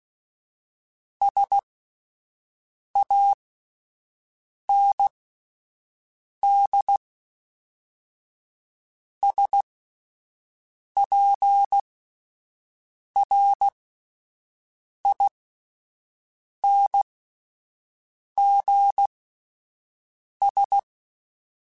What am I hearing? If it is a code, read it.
Morse code, heard as a keyed tone, about 16 wpm, spaced as for 5 wpm, 800 Hz. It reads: SAND SPRINGS